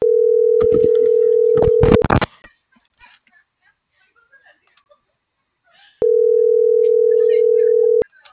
Ringback tone (North America), 440 + 480 Hz.